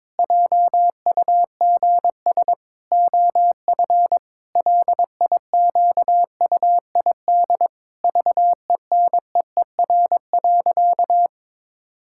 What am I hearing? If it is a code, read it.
Morse code, heard as a keyed tone, about 22 wpm, 700 Hz. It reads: JUGS OF LIQUID VENEER.
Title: Pangram Challenge v2 - 22wpm